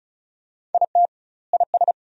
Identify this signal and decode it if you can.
Morse code, heard as a keyed tone, about 34 wpm, 700 Hz. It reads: IT IS
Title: 2x Word-Spacing - Sentences from Top 500 Words - 35wpm